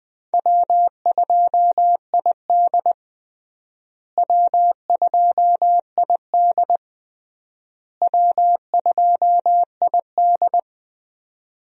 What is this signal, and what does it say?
Morse code, heard as a keyed tone, about 21 wpm, 700 Hz. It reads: W2ID W2ID W2ID
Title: USA Call Signs - 4 Characters - Repeated 3x - 20wpm